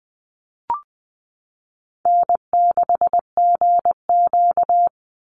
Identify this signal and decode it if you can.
Morse code, heard as a keyed tone, about 20 wpm, 700 Hz. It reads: N6GQ
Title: Easy Call Signs - 20wpm